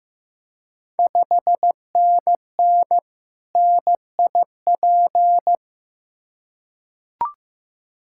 Morse code, heard as a keyed tone, about 15 wpm, 700 Hz. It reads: 5NN NIP
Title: Ontario QSO Party Exchange 15wpm